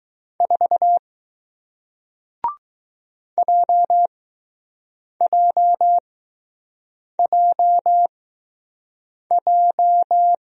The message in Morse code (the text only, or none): 4 JJJJ